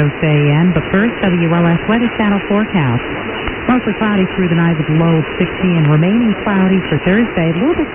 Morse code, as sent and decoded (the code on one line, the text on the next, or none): none